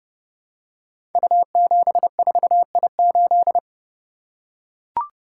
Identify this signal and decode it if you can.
Morse code, heard as a keyed tone, about 28 wpm, 700 Hz. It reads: U74I8
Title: Code Groups - 5 Letter:Numbers - 30wpm